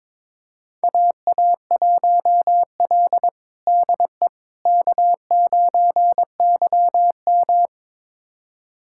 Morse code, heard as a keyed tone, about 22 wpm, 700 Hz. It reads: AA1L DE K9YM
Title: USA Callsign DE Callsign - 4 Characters - 22wpm